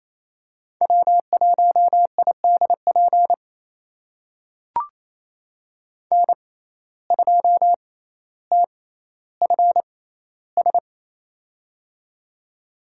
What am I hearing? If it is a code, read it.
Morse code, heard as a keyed tone, about 28 wpm, spaced as for 14 wpm, 700 Hz. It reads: W1IDP N2TFS
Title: USA Call Signs - 5 Characters - 6x Character Spacing - 28wpm